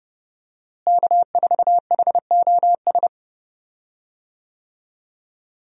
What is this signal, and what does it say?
Morse code, heard as a keyed tone, about 30 wpm, 700 Hz. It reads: K4HOS
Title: USA Call Signs - 5 Characters - Repeated 3x - 30wpm